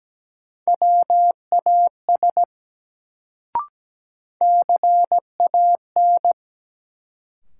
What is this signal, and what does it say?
Morse code, heard as a keyed tone, about 17 wpm, 700 Hz. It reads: WAS CAN
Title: Sentences from Top 100 Words - New Word Drill - 17wpm